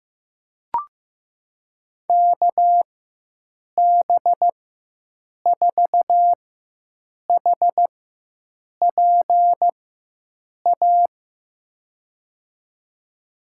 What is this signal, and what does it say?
Morse code, heard as a keyed tone, about 15 wpm, spaced as for 7 wpm, 700 Hz. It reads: KB4HPA
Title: USA Call Signs - 6 Characters - 4x Character Spacing - 15wpm